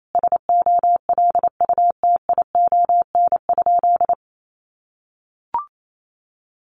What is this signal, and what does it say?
Morse code, heard as a keyed tone, about 28 wpm, 700 Hz. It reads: SOLUTION?
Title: Sentences from Top 1300 Words 28wpm